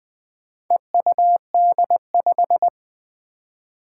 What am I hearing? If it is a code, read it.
Morse code, heard as a keyed tone, about 20 wpm, 700 Hz. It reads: EUD5
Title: Four Challenging Letter-Number Combinations 20wpm